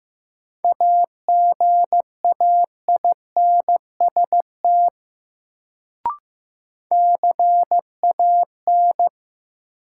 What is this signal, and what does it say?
Morse code, heard as a keyed tone, about 15 wpm, 700 Hz. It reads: AGAINST CAN
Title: Sets of 2 Words - New Word Drill - 15wpm